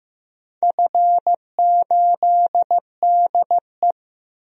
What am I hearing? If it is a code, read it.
Morse code, heard as a keyed tone, about 15 wpm, 700 Hz. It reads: F8DE
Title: DXCC-100 Call Signs - 4 Characters - Repeated 3x - 15wpm